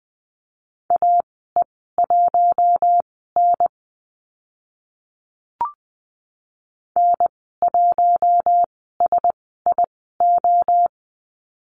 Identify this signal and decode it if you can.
Morse code, heard as a keyed tone, about 20 wpm, spaced as for 15 wpm, 700 Hz. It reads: AE1N N1SIO
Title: 042 Lesson 8g - Number 1 - Call Signs 20wpm